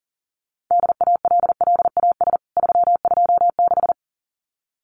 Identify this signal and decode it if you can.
Morse code, heard as a keyed tone, about 40 wpm, 700 Hz. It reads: DALLAS 326
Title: CWT Exchange 40wpm